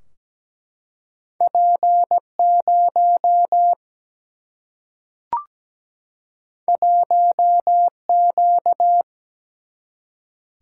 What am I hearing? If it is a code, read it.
Morse code, heard as a keyed tone, about 17 wpm, 700 Hz. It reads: P 0 1 Q
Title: Single-Letter Single-Number 17wpm